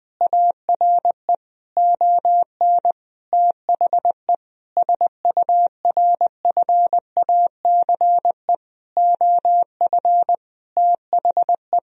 Morse code, heard as a keyed tone, about 20 wpm, 700 Hz. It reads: ARE ON THE SURFACE OF THE
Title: Sentences from Top 1000 Words - Review Part 3 - 20wpm